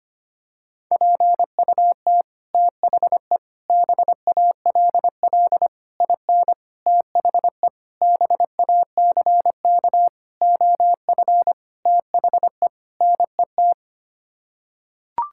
Morse code, heard as a keyed tone, about 25 wpm, 700 Hz. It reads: PUT THE BALL IN THE BACK OF THE NET
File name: Sets of 9 Words - 25wpm